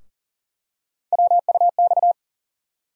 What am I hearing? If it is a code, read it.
Morse code, heard as a keyed tone, about 40 wpm, 700 Hz. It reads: WUX
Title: USA Call Sign Postfixes - 3 Letters 40wpm